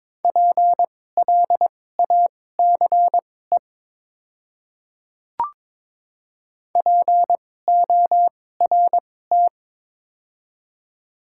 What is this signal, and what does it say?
Morse code, heard as a keyed tone, about 22 wpm, spaced as for 16 wpm, 700 Hz. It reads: PLACE PORT
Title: 2x Character Spacing - Top 300 Words - 22wpm